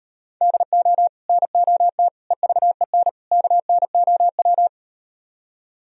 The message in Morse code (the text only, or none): DO NOT EVEN KNOW